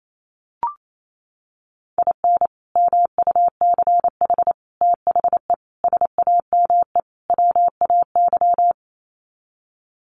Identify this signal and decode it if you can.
Morse code, heard as a keyed tone, about 28 wpm, 700 Hz. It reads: IN MUCH THE SAME WAY
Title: Sets of 5 Words Encore 28wpm